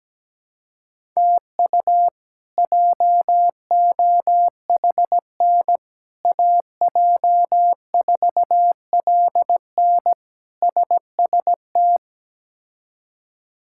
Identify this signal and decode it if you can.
Morse code, heard as a keyed tone, about 17 wpm, 700 Hz. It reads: TU JOHN AJ4LN SST
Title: K1USN SST QSO Practice - 17wpm